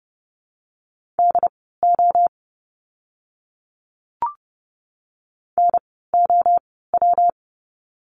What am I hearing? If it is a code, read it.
Morse code, heard as a keyed tone, about 30 wpm, spaced as for 17 wpm, 700 Hz. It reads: DO NOW
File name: course / 30:10wpm / 115 Lesson 19f - Letter w - Words 30wpm